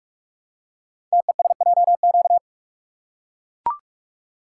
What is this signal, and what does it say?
Morse code, heard as a keyed tone, about 44 wpm, 700 Hz. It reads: TESJQ
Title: Code Groups - 5 Letters - 45wpm